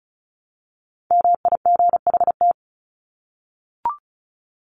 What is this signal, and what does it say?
Morse code, heard as a keyed tone, about 34 wpm, 700 Hz. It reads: MIGHT
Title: Top 500 Words 35wpm